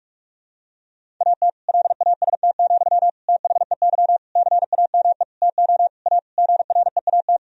Morse code, heard as a keyed tone, about 46 wpm, 700 Hz. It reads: AT LAST, THEY CAME TO A GREAT
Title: Sentences from Top 1000 Words - Review Part 1 - 45wpm